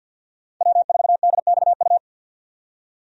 Morse code, heard as a keyed tone, about 52 wpm, 700 Hz. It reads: W4DXU
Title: USA Call Signs - 5 Characters - 50wpm